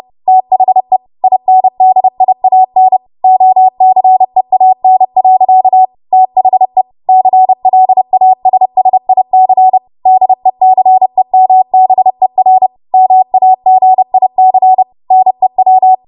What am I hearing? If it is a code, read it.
Morse code, heard as a keyed tone, about 30 wpm, 750 Hz. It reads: THE INDIAN OCEAN. THE CLASSIC DECEMBER MAGIC NEW
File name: Morse / 30WPM